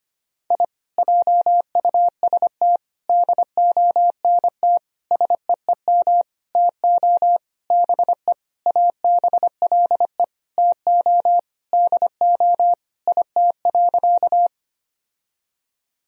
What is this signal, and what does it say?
Morse code, heard as a keyed tone, about 25 wpm, 700 Hz. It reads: I JUST DONT SEEM TO BE ABLE TO DO IT.
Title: Sentences from Top 1000 Words - Review Part 2 - 25wpm